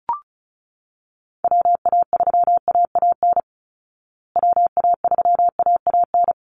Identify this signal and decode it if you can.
Morse code, heard as a keyed tone, about 34 wpm, 700 Hz. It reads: WA3AAN WA3AAN
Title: Call Signs Repeated 3x - 35wpm